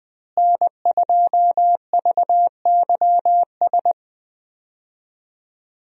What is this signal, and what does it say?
Morse code, heard as a keyed tone, about 20 wpm, 700 Hz. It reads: N2VYS
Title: USA Call Signs - 5 Characters - Repeated 2x - 20wpm